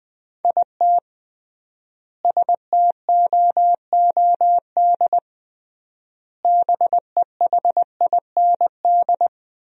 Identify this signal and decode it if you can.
Morse code, heard as a keyed tone, about 21 wpm, 700 Hz. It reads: IT STOOD BEHIND
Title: 3x Word-Spacing - Sentences from Top 500 Words - 20wpm